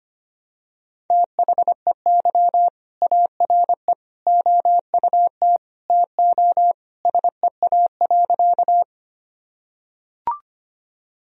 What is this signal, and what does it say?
Morse code, heard as a keyed tone, about 25 wpm, 700 Hz. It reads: THEY ARE OUT TO SEA.
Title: Sentences from Top 1000 Words - Review Part 1 - 25wpm